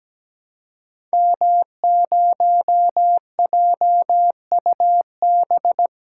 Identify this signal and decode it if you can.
Morse code, heard as a keyed tone, about 17 wpm, 700 Hz. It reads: M0JUB
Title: Code Groups - 5 Letter:Numbers - 17wpm